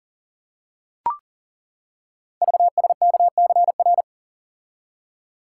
Morse code, heard as a keyed tone, about 42 wpm, 700 Hz. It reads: VSKCR
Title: Code Groups - 5 Letters - 40wpm